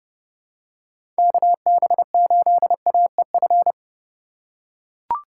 Morse code, heard as a keyed tone, about 30 wpm, 700 Hz. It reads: KB8AEF
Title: USA Call Signs - 6 Characters - 4x Character Spacing - 30wpm